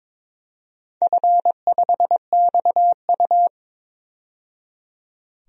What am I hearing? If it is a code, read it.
Morse code, heard as a keyed tone, about 22 wpm, 700 Hz. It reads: F5XU
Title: DXCC-100 Call Signs - 4 Characters - 22wpm